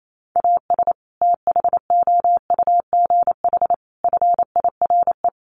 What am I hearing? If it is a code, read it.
Morse code, heard as a keyed tone, about 28 wpm, 700 Hz. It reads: AS THOUGH FIRE
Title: Sentences from Top 1500 Words 28wpm